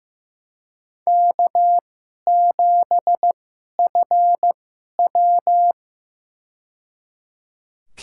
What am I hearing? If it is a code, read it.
Morse code, heard as a keyed tone, about 15 wpm, spaced as for 11 wpm, 700 Hz. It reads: K7FW
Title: USA Call Signs - 4 Characters - 2x Character Spacing - 15wpm